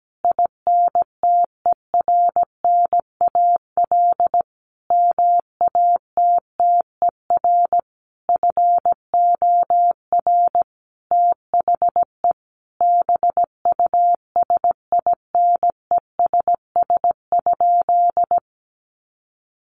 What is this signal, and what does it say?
Morse code, heard as a keyed tone, about 17 wpm, 700 Hz. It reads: INTERNAL MATTER FOR THE BUSINESS?
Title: Sentences from Top 1400 Words 17wpm